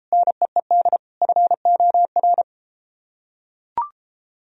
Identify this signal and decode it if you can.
Morse code, heard as a keyed tone, about 32 wpm, 700 Hz. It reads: NEED FOR